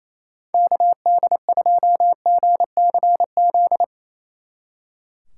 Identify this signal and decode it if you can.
Morse code, heard as a keyed tone, about 37 wpm, 700 Hz. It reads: KD2GCZ